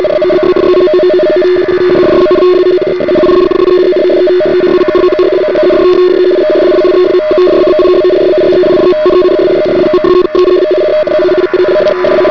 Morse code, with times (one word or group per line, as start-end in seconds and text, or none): none